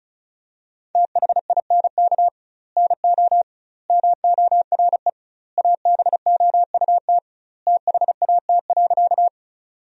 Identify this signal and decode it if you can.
Morse code, heard as a keyed tone, about 36 wpm, 700 Hz. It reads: THINK NO MORE ABOUT THAT.
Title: 2x Word-Spacing - Sentences from Top 200 Words - 35wpm